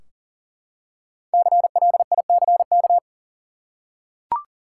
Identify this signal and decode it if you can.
Morse code, heard as a keyed tone, about 40 wpm, 700 Hz. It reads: CLICK
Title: Five Letter Words - 40wpm